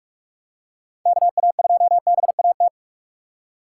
Morse code, heard as a keyed tone, about 46 wpm, 700 Hz. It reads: KA2BAT
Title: USA Call Signs - 6 Characters - 2x Character Spacing - 45wpm